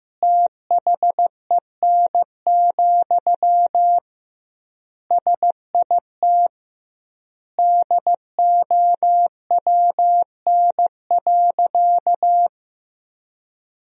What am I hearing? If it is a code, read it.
Morse code, heard as a keyed tone, about 15 wpm, 700 Hz. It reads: THEN, SIT DOWN.
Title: 2x Word-Spacing - Sentences from Top 500 Words - 15wpm